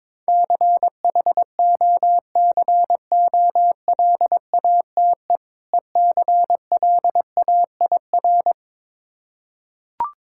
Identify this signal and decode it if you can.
Morse code, heard as a keyed tone, about 22 wpm, 690 Hz. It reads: CHOCOLATE ECLAIR
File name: Difficult Words with Hints - 22wpm